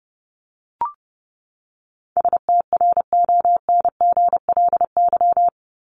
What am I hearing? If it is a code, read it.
Morse code, heard as a keyed tone, about 29 wpm, 700 Hz. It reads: STRONGLY
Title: LY Suffix - Top 100 Words with Base Words 30wpm